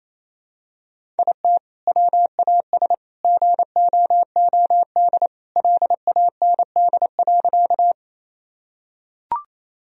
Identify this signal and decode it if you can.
Morse code, heard as a keyed tone, about 28 wpm, 700 Hz. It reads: IT WAS GOOD LAND.
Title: Sentences from Top 1000 Words - Review Part 1 - 28wpm